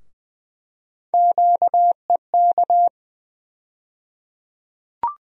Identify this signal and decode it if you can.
Morse code, heard as a keyed tone, about 20 wpm, 700 Hz. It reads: QEK
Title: USA Call Sign Postfixes - 3 Letters 20wpm